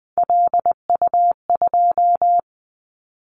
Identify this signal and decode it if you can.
Morse code, heard as a keyed tone, about 20 wpm, 700 Hz. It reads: LU2
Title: Callsign Prefixes for Top 100 DXCC - 20wpm